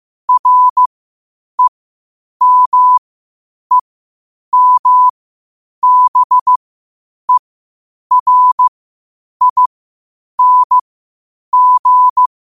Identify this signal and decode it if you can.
Morse code, heard as a keyed tone, about 15 wpm, spaced as for 8 wpm, 1000 Hz. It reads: REMEMBERING
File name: qotdfiles / 8wpm